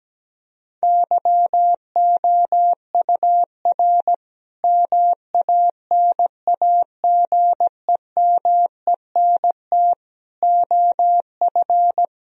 Morse code, heard as a keyed tone, about 17 wpm, 700 Hz. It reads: YOUR MANAGEMENT OF
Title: Sentences from Top 1000 Words - Review Part 4 - 17wpm